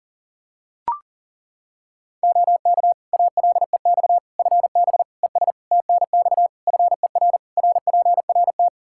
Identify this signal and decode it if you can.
Morse code, heard as a keyed tone, about 40 wpm, 700 Hz. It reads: OK ALEX FB ES TNX FER RPRT
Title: Practice QSO - Exchange 2 - Info - 40wpm